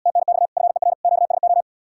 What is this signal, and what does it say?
Morse code, heard as a keyed tone, about 80 wpm, 700 Hz. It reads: TNX FER QSO